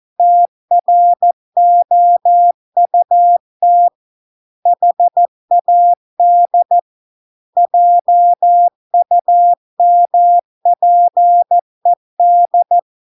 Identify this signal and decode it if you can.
Morse code, heard as a keyed tone, about 14 wpm, 700 Hz. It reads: TROUT HAD JUMPED